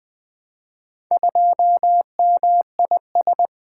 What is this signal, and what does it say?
Morse code, heard as a keyed tone, about 20 wpm, 700 Hz. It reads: 2MIS
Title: Four Letter-Number Combinations - 20wpm